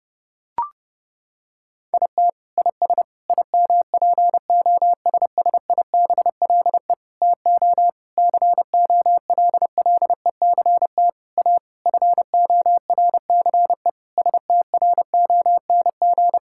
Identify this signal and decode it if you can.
Morse code, heard as a keyed tone, about 30 wpm, 700 Hz. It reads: IT IS IMPOSSIBLE TO COLLECT A FORCE STRONG
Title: Sentences from Top 1600 Words 30wpm